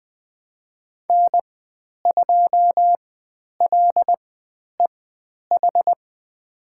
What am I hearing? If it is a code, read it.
Morse code, heard as a keyed tone, about 20 wpm, spaced as for 10 wpm, 700 Hz. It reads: N2LEH
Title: 095 Lesson 16g - Letter c - Call Signs 20wpm